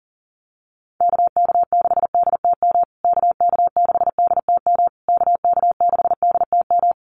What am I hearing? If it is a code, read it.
Morse code, heard as a keyed tone, about 40 wpm, 700 Hz. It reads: KK6DTM KK6DTM KK6DTM
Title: USA Call Signs Repeated 3x 40wpm